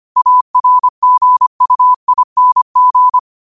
Morse code, heard as a keyed tone, about 25 wpm, 1000 Hz. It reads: ARGUING
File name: qotdfiles / 25wpm